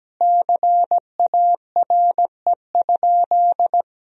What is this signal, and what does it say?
Morse code, heard as a keyed tone, about 17 wpm, 700 Hz. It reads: CARE?
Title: Sentences from Top 1000 Words - Review Part 3 - 17wpm